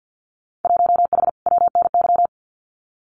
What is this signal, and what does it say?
Morse code, heard as a keyed tone, about 50 wpm, 700 Hz. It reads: 1H WNY
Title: Winter Field Day Exchanges Encore 50wpm